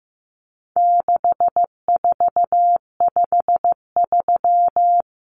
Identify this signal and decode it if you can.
Morse code, heard as a keyed tone, about 15 wpm, 700 Hz. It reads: 6453
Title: Numbers - 4-Digit 15wpm